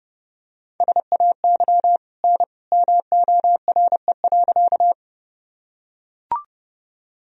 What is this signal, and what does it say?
Morse code, heard as a keyed tone, about 32 wpm, 700 Hz. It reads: SAY N MORE.
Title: Sentences from Top 200 Words and Abbreviations 30wpm